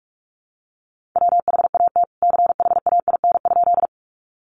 Morse code, heard as a keyed tone, about 44 wpm, 700 Hz. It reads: WHAT CHAIN?
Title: Sentences from Top 1500 Words 45wpm